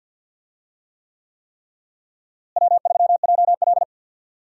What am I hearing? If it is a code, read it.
Morse code, heard as a keyed tone, about 51 wpm, 700 Hz. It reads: W3JL
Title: USA Call Signs - 4 Characters - Repeated 3x - 50wpm